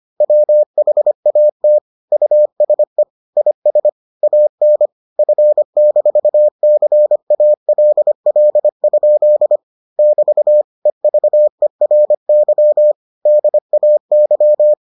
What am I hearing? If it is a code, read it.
Morse code, heard as a keyed tone, about 25 wpm, 600 Hz. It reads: WHAT USE IS AN F-CALL? <BT> EVERY DAY